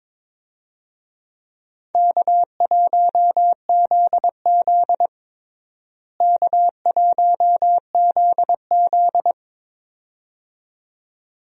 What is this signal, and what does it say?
Morse code, heard as a keyed tone, about 22 wpm, 700 Hz. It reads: K1ZZ K1ZZ
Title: USA Call Signs - 4 Characters - Repeated 2x - 22wpm